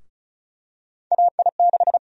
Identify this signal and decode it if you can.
Morse code, heard as a keyed tone, about 36 wpm, 700 Hz. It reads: AI6
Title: USA Call Sign Prefixes - 35wpm